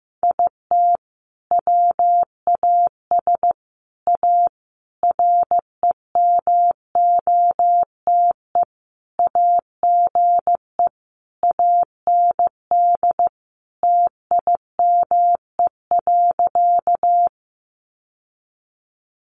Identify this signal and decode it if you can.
Morse code, heard as a keyed tone, about 15 wpm, 700 Hz. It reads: IT WAS A REMOTE AGE AND TIME.
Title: Sentences from Top 1300 Words 15wpm